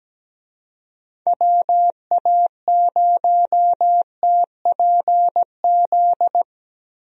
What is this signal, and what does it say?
Morse code, heard as a keyed tone, about 17 wpm, 700 Hz. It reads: WA0TPZ
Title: USA Call Signs - 6 Characters - Repeated 2x - 17wpm